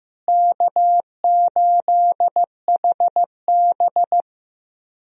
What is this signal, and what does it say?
Morse code, heard as a keyed tone, about 15 wpm, 700 Hz. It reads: K8HB
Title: USA Call Signs - 4 Characters - Repeated 2x - 15wpm